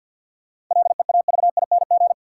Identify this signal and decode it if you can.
Morse code, heard as a keyed tone, about 47 wpm, 700 Hz. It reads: LEAVING
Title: Sentences from Top 1000 Words - Review Part 4 - New Word Drill - 50wpm